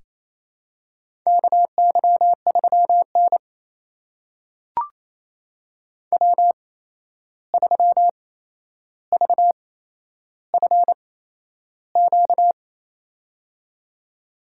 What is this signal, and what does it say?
Morse code, heard as a keyed tone, about 28 wpm, spaced as for 11 wpm, 700 Hz. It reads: KY3N W3VFQ
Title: USA Call Signs - 8x Character Spacing - 28wpm